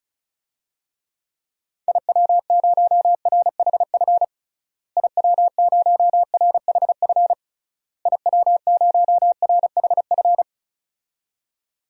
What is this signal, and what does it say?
Morse code, heard as a keyed tone, about 36 wpm, 700 Hz. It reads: IW0RHF IW0RHF IW0RHF
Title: DXCC-100 Call Signs - 6 Characters - Repeated 3x - 35wpm